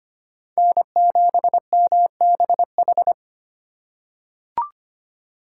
Morse code, heard as a keyed tone, about 25 wpm, 700 Hz. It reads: N7MBH
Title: USA Call Signs - 5 Characters - 4x Character Spacing - 25wpm